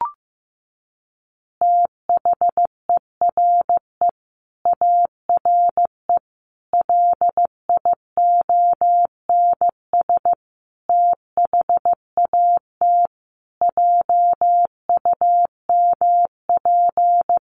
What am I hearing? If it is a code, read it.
Morse code, heard as a keyed tone, about 15 wpm, 700 Hz. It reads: THERE ARE LIONS THAT JUMP
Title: Easy Phrases 15wpm